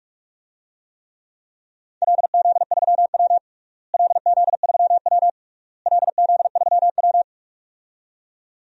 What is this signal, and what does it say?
Morse code, heard as a keyed tone, about 46 wpm, 700 Hz. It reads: LZ3W LZ3W LZ3W
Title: DXCC-100 Call Signs - 4 Characters - Repeated 3x - 45wpm